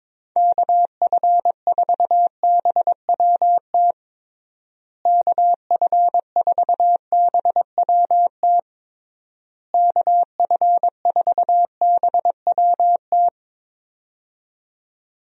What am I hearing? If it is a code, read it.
Morse code, heard as a keyed tone, about 22 wpm, 700 Hz. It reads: KF4BWT KF4BWT KF4BWT
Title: USA Call Signs - 6 Characters - Repeated 3x - 22wpm